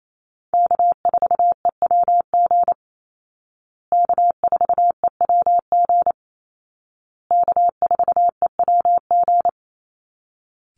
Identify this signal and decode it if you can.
Morse code, heard as a keyed tone, about 28 wpm, 700 Hz. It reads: K4EWG K4EWG K4EWG
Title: Call Signs Repeated 2x - 28wpm